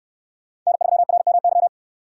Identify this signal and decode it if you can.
Morse code, heard as a keyed tone, about 69 wpm, 700 Hz. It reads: N3RMY